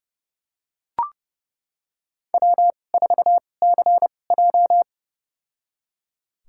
Morse code, heard as a keyed tone, about 30 wpm, spaced as for 22 wpm, 700 Hz. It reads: W4CJ
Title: USA Call Signs - 4 Characters - 2x Character Spacing - 30wpm